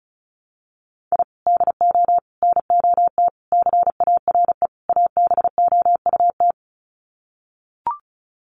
Morse code, heard as a keyed tone, about 34 wpm, 700 Hz. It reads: I DO NOT CARE ABOUT
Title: Sets of 5 Words Encore2 35wpm